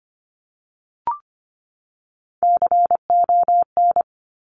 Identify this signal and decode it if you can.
Morse code, heard as a keyed tone, about 25 wpm, 700 Hz. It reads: CON
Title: Top Three Letter Combinations in Words 25wpm